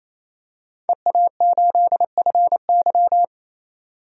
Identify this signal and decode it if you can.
Morse code, heard as a keyed tone, about 28 wpm, 700 Hz. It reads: EA8FY
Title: DXCC-100 Call Signs - 5 Characters - 28wpm